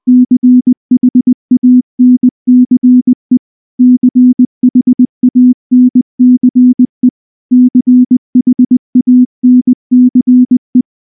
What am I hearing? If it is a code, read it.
Morse code, heard as a keyed tone, about 19 wpm, 250 Hz. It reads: CHANCE CHANCE CHANCE
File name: morse / morse-training-20WPM-10